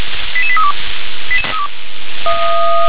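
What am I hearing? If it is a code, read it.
Five-tone selective call (ZVEI), heard as quick runs of five tones, tones of about 70 ms. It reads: 88832
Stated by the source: ZVEI five-tone selective call 88863, received on an SDR.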